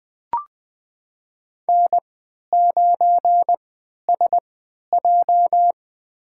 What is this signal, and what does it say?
Morse code, heard as a keyed tone, about 21 wpm, spaced as for 12 wpm, 700 Hz. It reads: N9SJ
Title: USA Call Signs - 4 Characters - 3x Character Spacing - 20wpm